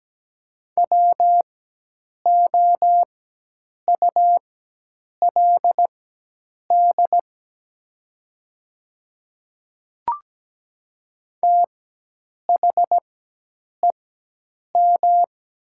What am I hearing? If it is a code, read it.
Morse code, heard as a keyed tone, about 17 wpm, spaced as for 8 wpm, 700 Hz. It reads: WOULD THEM
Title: 4x Character Spacing - Top 100 Words - 17wpm